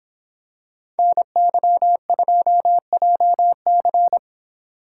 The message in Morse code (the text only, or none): NY2JC